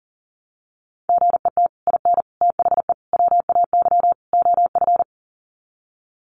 Morse code, heard as a keyed tone, about 40 wpm, 700 Hz. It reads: GET IN THE WAY OF
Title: Sets of 5 Words Encore 40wpm